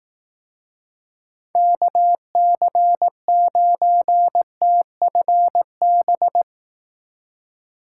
Morse code, heard as a keyed tone, about 18 wpm, 700 Hz. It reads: KC9TFB
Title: Speed Racing - USA Call Signs 15wpm